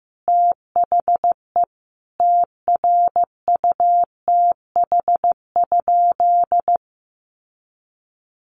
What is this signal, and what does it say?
Morse code, heard as a keyed tone, about 15 wpm, 700 Hz. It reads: THE TRUTH?
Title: Sentences from Top 1300 Words 15wpm